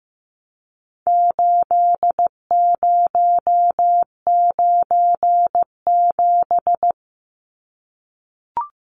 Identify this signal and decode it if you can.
Morse code, heard as a keyed tone, about 15 wpm, 700 Hz. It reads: 8097
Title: Numbers - 4-Digit 15wpm